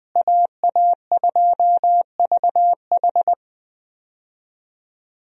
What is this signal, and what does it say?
Morse code, heard as a keyed tone, about 20 wpm, 700 Hz. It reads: AA2VH
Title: USA Call Signs - 5 Characters - Repeated 3x - 20wpm